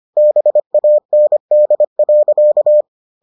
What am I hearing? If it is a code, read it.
Morse code, heard as a keyed tone, about 25 wpm, 600 Hz. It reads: BAND.